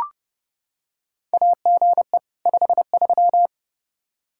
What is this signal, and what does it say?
Morse code, heard as a keyed tone, about 30 wpm, 700 Hz. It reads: AGE 53
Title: Practice QSO - Exchange 3 - More Info - 30wpm